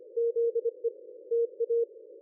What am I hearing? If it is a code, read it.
Morse code, heard as a keyed tone, about 25 wpm, 460 Hz. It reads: ZE TA
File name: dlazaawansowanych / nagraniazpasma / ExpertSDR2 line out 23.01.2020 14.04.01